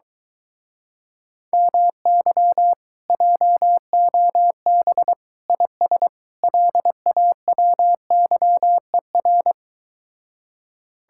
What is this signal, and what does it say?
Morse code, heard as a keyed tone, about 24 wpm, 700 Hz. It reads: MY JOB IS LAWYER